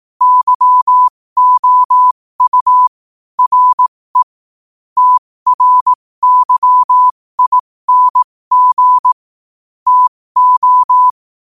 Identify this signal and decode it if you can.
Morse code, heard as a keyed tone, about 18 wpm, spaced as for 15 wpm, 1000 Hz. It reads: YOU RE TRYING TO
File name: qotdfiles / 15wpm